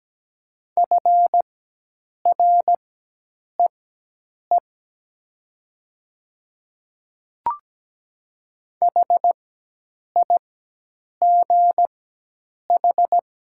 Morse code, heard as a keyed tone, about 17 wpm, spaced as for 8 wpm, 700 Hz. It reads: FREE HIGH
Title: 4x Character Spacing - Top 500 Words - 17wpm